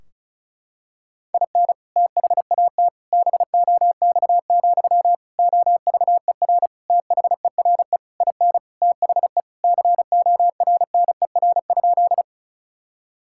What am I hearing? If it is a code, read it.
Morse code, heard as a keyed tone, about 34 wpm, 700 Hz. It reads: IN THAT BOX, OVER THERE IN THE CORNER?
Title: Sentences from Top 1600 Words 35wpm